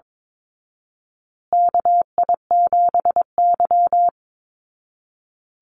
Morse code, heard as a keyed tone, about 22 wpm, 700 Hz. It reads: KI7Y
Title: Call Signs Repeated 2x - 22wpm